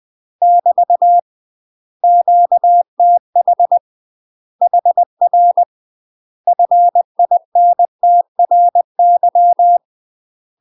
Morse code, heard as a keyed tone, about 20 wpm, 700 Hz. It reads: <BT> QTH HR FINTRY
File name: UK MASTER QSOs TRAINING / 20WPM / UK MASTER QSOs 0013